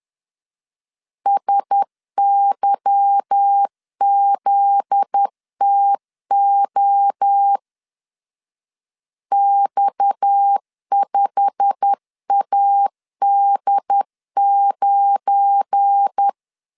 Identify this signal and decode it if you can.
Morse code, heard as a keyed tone, about 10 wpm, 800 Hz. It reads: SYZTO X5AD9
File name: LEZIONI / 8 YJ45 / 8 YJ45 10wpm 2549